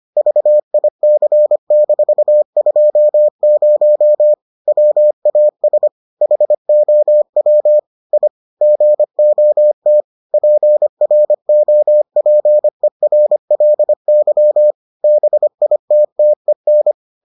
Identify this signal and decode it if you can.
Morse code, heard as a keyed tone, about 25 wpm, 600 Hz. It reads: VIC-20 WAS HOW I GOT PROPERLY BITTEN